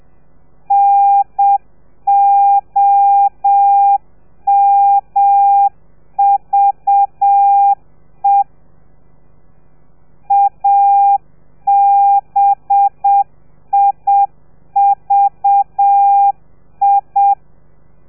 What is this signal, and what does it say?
Morse code, heard as a keyed tone, about 7 wpm, 800 Hz. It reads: NOMVE ABIVI
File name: telegrafia / i2tfj / Lez4 30